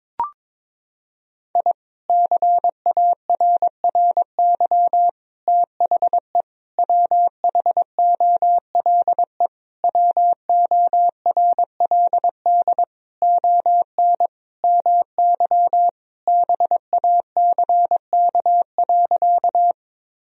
Morse code, heard as a keyed tone, about 21 wpm, 700 Hz. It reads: I CARRY THE WHOLE WORLD ON MY BACK.
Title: Sentences from Top 1000 Words - Review Part 1 - 22wpm